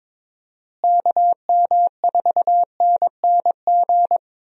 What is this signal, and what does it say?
Morse code, heard as a keyed tone, about 22 wpm, 700 Hz. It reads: KM4NNG
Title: USA Call Signs - 6 Characters - Repeated 2x - 22wpm